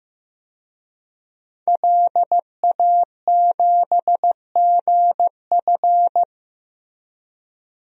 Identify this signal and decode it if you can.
Morse code, heard as a keyed tone, about 15 wpm, 700 Hz. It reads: LA7GF
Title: DXCC-100 Call Signs - 5 Characters - Repeated 3x - 15wpm